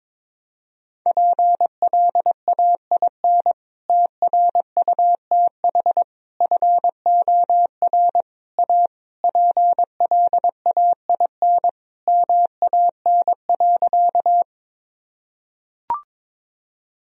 Morse code, heard as a keyed tone, about 22 wpm, 700 Hz. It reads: PLAIN TRUTH FOR A PLAIN MAN.
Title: Sentences from Top 1000 Words - Review Part 2 - 22wpm